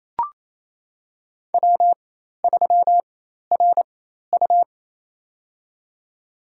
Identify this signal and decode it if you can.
Morse code, heard as a keyed tone, about 28 wpm, spaced as for 13 wpm, 700 Hz. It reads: W3RU
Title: USA Call Signs - 4x Character Spacing - 28wpm